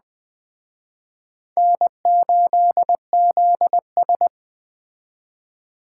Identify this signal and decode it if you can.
Morse code, heard as a keyed tone, about 21 wpm, 700 Hz. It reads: N8ZS